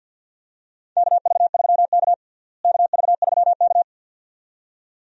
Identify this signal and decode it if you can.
Morse code, heard as a keyed tone, about 47 wpm, 700 Hz. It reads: KV3K KV3K
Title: USA Call Signs - 4 Characters - Repeated 2x - 50wpm